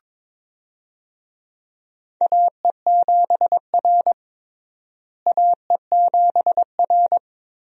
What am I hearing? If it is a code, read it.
Morse code, heard as a keyed tone, about 21 wpm, 700 Hz. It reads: AE7R AE7R
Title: USA Call Signs - 4 Characters - Repeated 2x - 22wpm